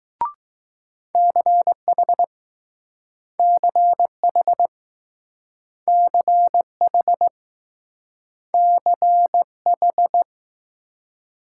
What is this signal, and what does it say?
Morse code, heard as a keyed tone, about 20 wpm, 700 Hz. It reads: CH CH CH CH